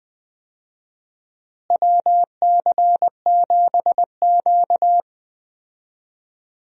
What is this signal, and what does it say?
Morse code, heard as a keyed tone, about 20 wpm, 700 Hz. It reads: WC7Q
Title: USA Call Signs - 4 Characters - Repeated 2x - 20wpm